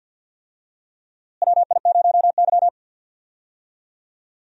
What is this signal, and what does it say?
Morse code, heard as a keyed tone, about 49 wpm, 700 Hz. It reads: WI0Y
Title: USA Call Signs - 4 Characters - Repeated 2x - 50wpm